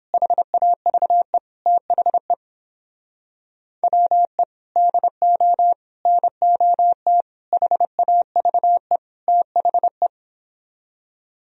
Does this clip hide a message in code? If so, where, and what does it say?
Yes: Morse code from 0.1 to 10.1 s, heard as a keyed tone, about 31 wpm, 700 Hz. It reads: HAVE THE WE DO NOT HAVE THE